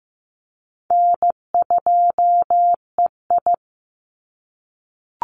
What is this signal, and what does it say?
Morse code, heard as a keyed tone, about 15 wpm, 700 Hz. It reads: N2EI
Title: Easy Call Signs - 15wpm